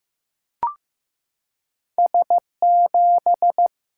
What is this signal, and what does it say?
Morse code, heard as a keyed tone, about 15 wpm, 700 Hz. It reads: S7
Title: Single-Letter Single-Number 15wpm